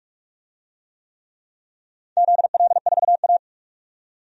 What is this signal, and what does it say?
Morse code, heard as a keyed tone, about 46 wpm, 700 Hz. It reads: ZL4A
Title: DXCC-100 Call Signs - 4 Characters - 45wpm